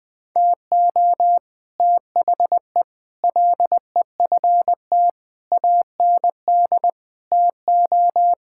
Morse code, heard as a keyed tone, about 20 wpm, 700 Hz. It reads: TO THE LEFT AND TO
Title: Sentences with 9-Word Phrases - 20wpm